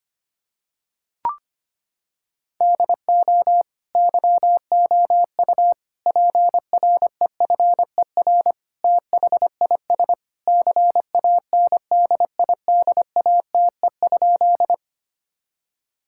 Morse code, heard as a keyed tone, about 25 wpm, 700 Hz. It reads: DO YOU PREFER THIS CANDIDATE?
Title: Sentences from Top 1600 Words 25wpm